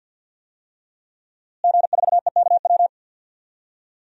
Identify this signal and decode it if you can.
Morse code, heard as a keyed tone, about 50 wpm, 700 Hz. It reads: G4EKW
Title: DXCC-100 Call Signs - 5 Characters - Repeated 3x - 50wpm